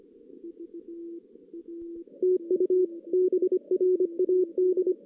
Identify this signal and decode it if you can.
Morse code, heard as a keyed tone, about 25 wpm, 380 Hz. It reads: TU BRAD
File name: 2016 JAN CW NAQP